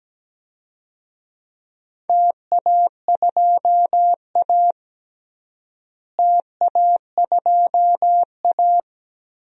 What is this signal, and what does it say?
Morse code, heard as a keyed tone, about 17 wpm, 700 Hz. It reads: TA2A TA2A
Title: DXCC-100 Call Signs - 4 Characters - Repeated 2x - 17wpm